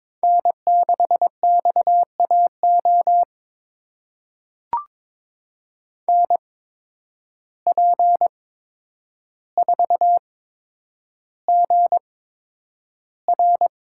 Morse code, heard as a keyed tone, about 22 wpm, spaced as for 9 wpm, 700 Hz. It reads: N6XAO NP4GR
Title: USA Call Signs - 5 Characters - 8x Character Spacing - 22wpm